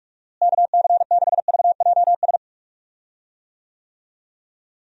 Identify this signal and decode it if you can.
Morse code, heard as a keyed tone, about 46 wpm, 700 Hz. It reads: KC6VJS
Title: USA Call Signs - 6 Characters - Repeated 2x - 45wpm